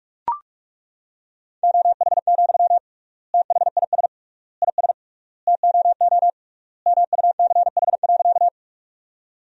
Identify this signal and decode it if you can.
Morse code, heard as a keyed tone, about 43 wpm, 700 Hz. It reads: OH, THIS IS TOO MUCH.
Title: 3x Word-Spacing - Sentences from Top 500 Words - 45wpm